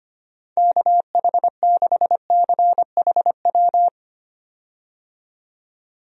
Morse code, heard as a keyed tone, about 25 wpm, 700 Hz. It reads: KH6CHW
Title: USA Call Signs - 6 Characters - 25wpm